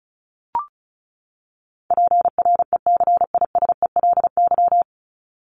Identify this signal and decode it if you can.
Morse code, heard as a keyed tone, about 34 wpm, 700 Hz. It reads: PRECISELY
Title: LY Suffix - Top 100 Words with Base Words 35wpm